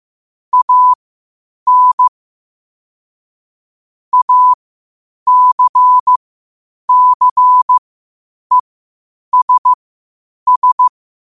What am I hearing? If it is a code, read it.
Morse code, heard as a keyed tone, about 15 wpm, spaced as for 8 wpm, 1000 Hz. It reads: AN ACCESS
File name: qotdfiles / 8wpm